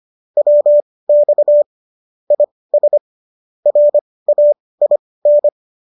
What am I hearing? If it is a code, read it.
Morse code, heard as a keyed tone, about 25 wpm, spaced as for 18 wpm, 590 Hz. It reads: WX IS RAIN